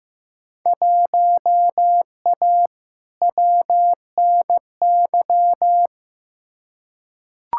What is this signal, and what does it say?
Morse code, heard as a keyed tone, about 15 wpm, 700 Hz. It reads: 1A WNY
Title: Field Day Exchanges Encore - 15wpm